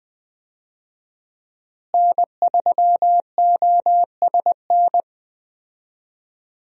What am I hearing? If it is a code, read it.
Morse code, heard as a keyed tone, about 20 wpm, 700 Hz. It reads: N3OSN
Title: USA Call Signs - 5 Characters - Repeated 2x - 20wpm